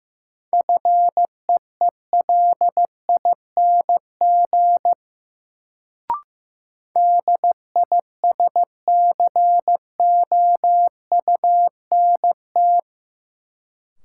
Morse code, heard as a keyed tone, about 15 wpm, 700 Hz. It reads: FEELING DISCOUNT
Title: Sentences from Top 1200 Words - New Word Drill Alt - 15wpm